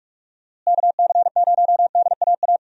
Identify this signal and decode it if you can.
Morse code, heard as a keyed tone, about 46 wpm, 700 Hz. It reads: KC0DAA
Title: USA Call Signs - 6 Characters - Repeated 3x - 45wpm